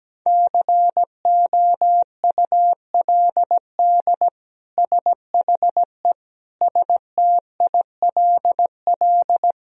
Morse code, heard as a keyed tone, about 17 wpm, 700 Hz. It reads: COULD SHE STILL
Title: Sentences from Top 1000 Words - Review Part 1 - 17wpm